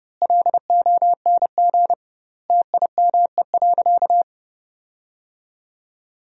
Morse code, heard as a keyed tone, about 30 wpm, 700 Hz. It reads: LONG TIME.
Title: 2x Word-Spacing - Sentences from Top 400 Words - 30wpm